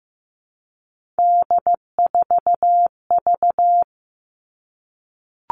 Morse code, heard as a keyed tone, about 15 wpm, 700 Hz. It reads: D4V
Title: All Cut Numbers - 3-Digit 15wpm